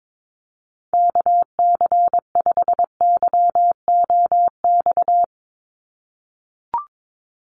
Morse code, heard as a keyed tone, about 22 wpm, 700 Hz. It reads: KC5YOX
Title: USA Call Signs 22wpm